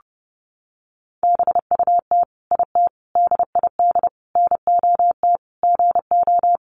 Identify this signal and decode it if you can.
Morse code, heard as a keyed tone, about 30 wpm, 700 Hz. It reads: BUT IT DID NOT GO
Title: Sentences from Top 600 Words 30wpm